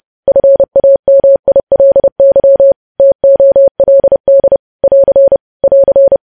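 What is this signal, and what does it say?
Morse code, heard as a keyed tone, about 30 wpm, 550 Hz. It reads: FAMILY TOLD <AR> <AR>
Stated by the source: Morse code news headlines practice at 30 WPM.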